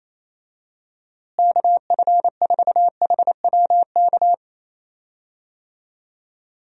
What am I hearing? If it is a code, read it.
Morse code, heard as a keyed tone, about 28 wpm, 700 Hz. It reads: KF4HWK
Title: USA Call Signs - 6 Characters - Repeated 2x - 28wpm